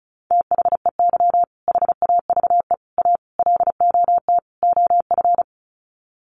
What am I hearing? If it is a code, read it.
Morse code, heard as a keyed tone, about 34 wpm, 700 Hz. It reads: THEY HAVE A LOT OF
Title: Sets of 5 Words Encore2 35wpm